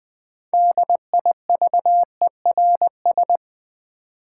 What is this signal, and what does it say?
Morse code, heard as a keyed tone, about 20 wpm, 700 Hz. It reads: DIVERS
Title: Build A Word - 3 Letters At A Time - 9-Letter Words - 20wpm